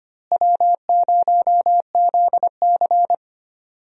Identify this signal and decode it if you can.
Morse code, heard as a keyed tone, about 25 wpm, 700 Hz. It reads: W0ZC
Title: USA Call Signs - 4 Characters - Repeated 3x - 25wpm